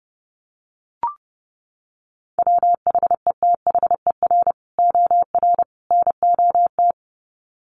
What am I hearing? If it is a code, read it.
Morse code, heard as a keyed tone, about 29 wpm, 700 Hz. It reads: WHETHER OR NOT
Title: Sets of 3 Words 30wpm